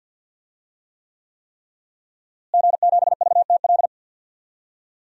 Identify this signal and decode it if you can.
Morse code, heard as a keyed tone, about 46 wpm, 700 Hz. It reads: G7VTL
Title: DXCC-100 Call Signs - 5 Characters - Repeated 3x - 50wpm